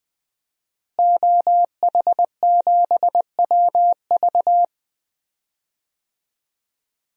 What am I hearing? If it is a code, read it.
Morse code, heard as a keyed tone, about 20 wpm, 700 Hz. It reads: OH7WV
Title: DXCC-100 Call Signs - 5 Characters - Repeated 3x - 20wpm